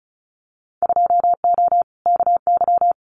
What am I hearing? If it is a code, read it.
Morse code, heard as a keyed tone, about 36 wpm, 700 Hz. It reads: 2O KY
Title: Winter Field Day Exchanges 35wpm